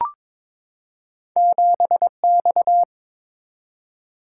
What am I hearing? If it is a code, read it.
Morse code, heard as a keyed tone, about 22 wpm, 700 Hz. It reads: 7X
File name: practice / 22wpm / Two Letter-Number Combinations - 22wpm